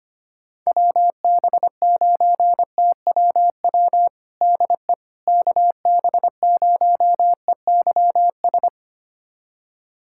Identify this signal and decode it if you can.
Morse code, heard as a keyed tone, about 25 wpm, 700 Hz. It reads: WB9TWW DE KB0EYS
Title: USA Callsign DE Callsign - 6 Characters - 25wpm